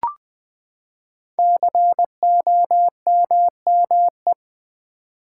Morse code, heard as a keyed tone, about 20 wpm, 700 Hz. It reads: COMME
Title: Build A Word - 5 Letters At A Time - 10-Letter Words - 20wpm